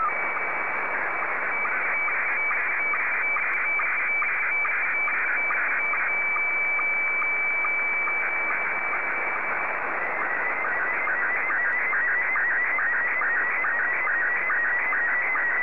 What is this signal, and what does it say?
SSTV (slow-scan television), Scottie 1.
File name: sstv / S1-VK7ZCF